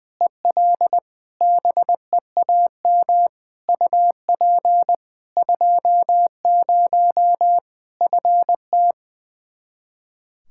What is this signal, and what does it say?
Morse code, heard as a keyed tone, about 21 wpm, 700 Hz. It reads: EL BEAM UP 20 FT
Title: Practice QSO - Exchange 2 - Info - 20wpm